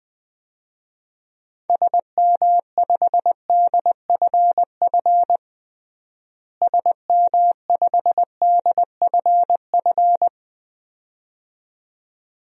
Morse code, heard as a keyed tone, about 20 wpm, 700 Hz. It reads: SM5DFF SM5DFF
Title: DXCC-100 Call Signs - 6 Characters - Repeated 2x - 20wpm